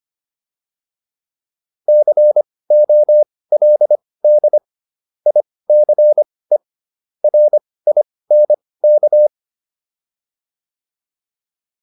Morse code, heard as a keyed tone, about 25 wpm, spaced as for 18 wpm, 600 Hz. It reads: COLD ICE RINK
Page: intermediate-practice-files